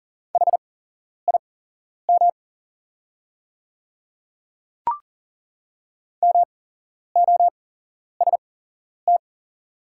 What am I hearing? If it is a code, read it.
Morse code, heard as a keyed tone, about 39 wpm, spaced as for 11 wpm, 700 Hz. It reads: HIM MOST
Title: 8x Character Spacing - Top 100 Words - 40wpm